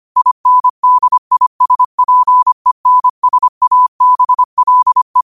Morse code, heard as a keyed tone, about 25 wpm, 1000 Hz. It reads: INDISPENSABLE
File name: qotdfiles / 25wpm